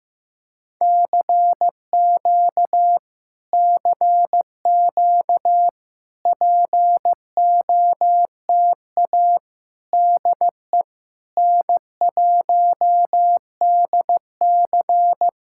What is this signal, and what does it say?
Morse code, heard as a keyed tone, about 15 wpm, 700 Hz. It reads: CQ CQ POTA DE N1DC
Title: Speed-Racing Common QSO Phrases Encore - 15wpm